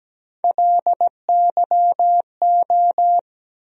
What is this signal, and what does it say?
Morse code, heard as a keyed tone, about 17 wpm, 700 Hz. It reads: LYO
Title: USA Call Sign Postfixes - 3 Letters 17wpm